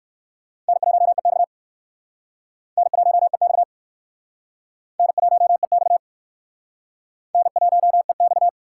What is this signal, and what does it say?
Morse code, heard as a keyed tone, about 60 wpm, 700 Hz. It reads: N1EX N1EX N1EX N1EX